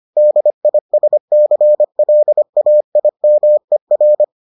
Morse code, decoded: DISCLAIMER

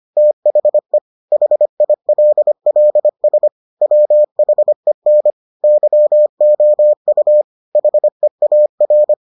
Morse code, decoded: THE HILLS WHEN YOU HEAR

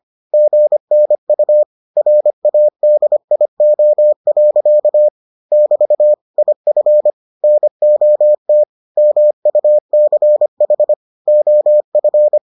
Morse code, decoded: GNU RADIO. <BT> IF NOT MUCH OF

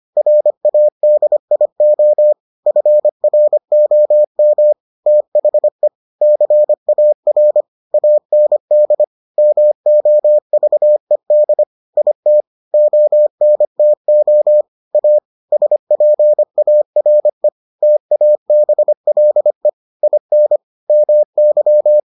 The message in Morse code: RADIO FROM THE CAR AND MOVED IT ONTO A SPARE TABLE IN MY

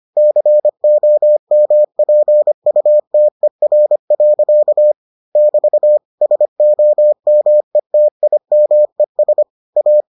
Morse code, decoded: COMPUTER. <BT> SOMETIMES A

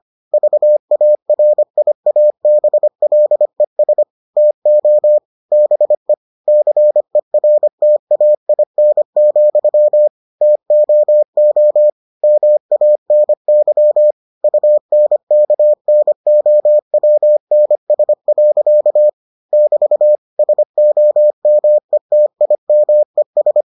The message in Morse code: VARIABLES TO BE CERTAIN, TOO MANY UNKNOWNS. <BT> SOMETIMES

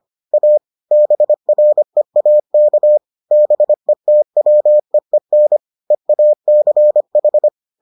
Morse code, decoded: A BREAK BETWEEN EACH

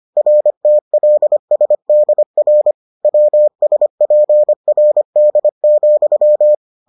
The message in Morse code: RTLSDR WSPRD,

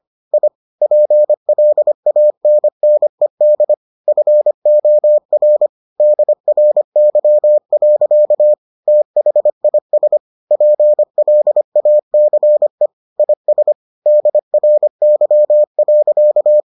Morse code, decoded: I PLANNED FOR DRY. THIS PLACE IS DRY.

